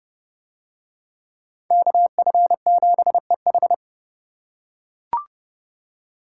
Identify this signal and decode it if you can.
Morse code, heard as a keyed tone, about 30 wpm, 700 Hz. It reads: KF7EH